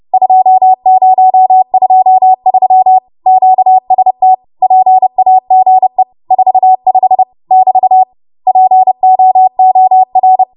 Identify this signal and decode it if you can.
Morse code, heard as a keyed tone, about 30 wpm, 750 Hz. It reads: 2023 QST PAGE 45 <BT> POOR
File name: Morse / 30WPM